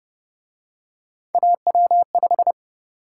Morse code, heard as a keyed tone, about 30 wpm, 700 Hz. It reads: AW5
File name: Speed-Racing - Three Letter-Number Combinations - 25wpm